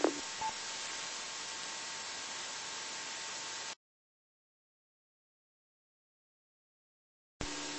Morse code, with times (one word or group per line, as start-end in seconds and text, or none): none